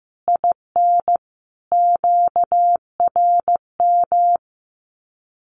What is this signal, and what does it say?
Morse code, heard as a keyed tone, about 15 wpm, 700 Hz. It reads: IN QRM
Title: Common QSO Phrases 15wpm